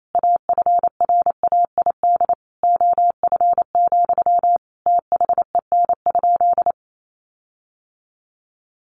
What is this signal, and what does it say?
Morse code, heard as a keyed tone, about 28 wpm, 700 Hz. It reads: AFRAID OF, THEN?
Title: Sentences from Top 1500 Words 28wpm